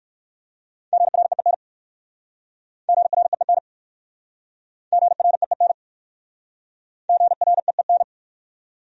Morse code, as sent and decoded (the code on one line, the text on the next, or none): --. .-. . -. / --. .-. . -. / --. .-. . -. / --. .-. . -.
GREEN GREEN GREEN GREEN